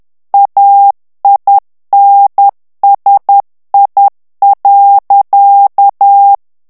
Morse code, heard as a keyed tone, about 10 wpm, 800 Hz. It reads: AINSI.
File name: morse mp3 / FRANCAIS TEXTE CLAIR 2 VITESSE 10 MOTS MINUTES